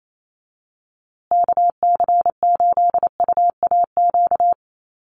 Morse code, heard as a keyed tone, about 28 wpm, 700 Hz. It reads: KC8UAQ
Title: USA Call Signs Repeated 3x 28wpm